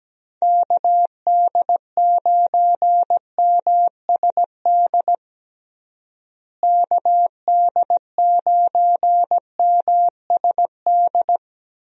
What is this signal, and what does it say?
Morse code, heard as a keyed tone, about 17 wpm, 700 Hz. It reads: KD9MSD KD9MSD
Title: USA Call Signs - 6 Characters - Repeated 2x - 17wpm